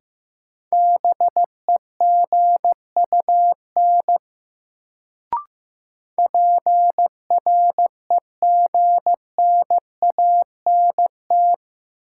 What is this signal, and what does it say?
Morse code, heard as a keyed tone, about 15 wpm, 700 Hz. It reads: BEGUN PREGNANT
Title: Sentences from Top 1000 Words - Review Part 6 - New Word Drill - 15wpm